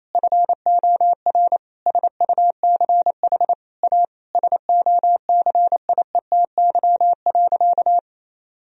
Morse code, heard as a keyed tone, about 27 wpm, 700 Hz. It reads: FOR SUCH A SOCIETY.
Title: Sentences from Top 1000 Words - Review Part 4 - 28wpm